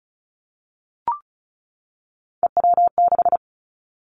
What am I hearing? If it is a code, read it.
Morse code, heard as a keyed tone, about 36 wpm, 700 Hz. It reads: EW6
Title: Three Challenging Letter-Number Combinations 35wpm